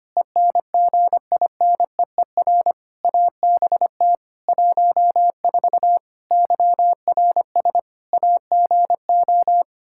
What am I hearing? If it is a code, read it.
Morse code, heard as a keyed tone, about 25 wpm, 700 Hz. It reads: ENGINEER ABT 14 YRS AGO
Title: K1USN SST QSO Practice 25wpm